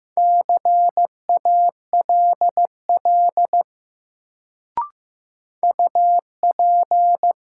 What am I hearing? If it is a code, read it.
Morse code, heard as a keyed tone, about 15 wpm, 700 Hz. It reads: CALL UP
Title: Sentences from Top 100 Words - New Word Drill - 15wpm